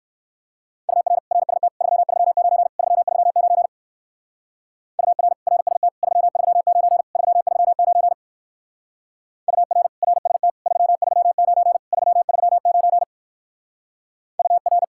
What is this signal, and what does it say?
Morse code, heard as a keyed tone, about 59 wpm, 700 Hz. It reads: UR RST 339 339 UR RST 339 339 UR RST 339 339 UR